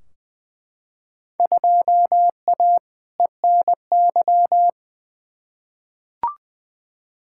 Morse code, heard as a keyed tone, about 20 wpm, 700 Hz. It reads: 2A ENY
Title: Field Day Exchanges Encore - 20wpm